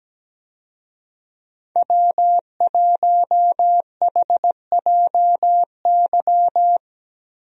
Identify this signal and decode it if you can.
Morse code, heard as a keyed tone, about 17 wpm, 700 Hz. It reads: W1HJY
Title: USA Call Signs - 5 Characters - Repeated 2x - 17wpm